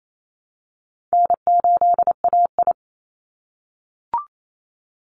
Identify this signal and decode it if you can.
Morse code, heard as a keyed tone, about 28 wpm, 700 Hz. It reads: N8AI
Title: Easy Call Signs - 28wpm